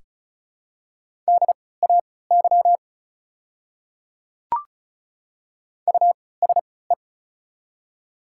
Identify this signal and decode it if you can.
Morse code, heard as a keyed tone, about 36 wpm, spaced as for 20 wpm, 700 Hz. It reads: DAY USE
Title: 3x Character Spacing - Top 200 Words - 35wpm